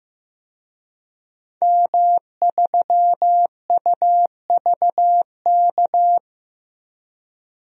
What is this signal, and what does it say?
Morse code, heard as a keyed tone, about 15 wpm, 700 Hz. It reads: M3UVK
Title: DXCC-100 Call Signs - 5 Characters - 15wpm